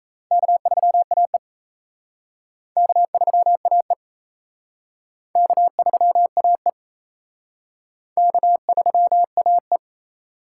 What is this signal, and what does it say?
Morse code, heard as a keyed tone, about 37 wpm, 700 Hz. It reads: K3AE K3AE K3AE K3AE